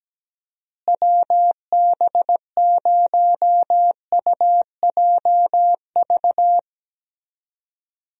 Morse code, heard as a keyed tone, about 17 wpm, 700 Hz. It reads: WB0UJV
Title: USA Call Signs - 6 Characters - Repeated 2x - 17wpm